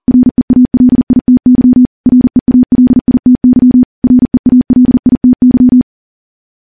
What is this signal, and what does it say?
Morse code, heard as a keyed tone, about 40 wpm, 250 Hz. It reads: REALITY REALITY REALITY
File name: morse / morse-training-40WPM